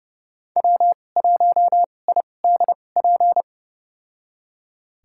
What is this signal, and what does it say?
Morse code, heard as a keyed tone, about 30 wpm, spaced as for 22 wpm, 700 Hz. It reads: W1IDP
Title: USA Call Signs - 2x Character Spacing - 30wpm